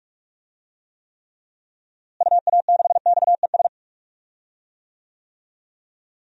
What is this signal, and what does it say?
Morse code, heard as a keyed tone, about 42 wpm, 700 Hz. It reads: UA6XES
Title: DXCC-100 Call Signs - 6 Characters - Repeated 2x - 45wpm